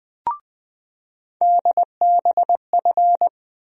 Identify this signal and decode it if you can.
Morse code, heard as a keyed tone, about 21 wpm, 700 Hz. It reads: DBF
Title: USA Call Sign Postfixes - 3 Letters 20wpm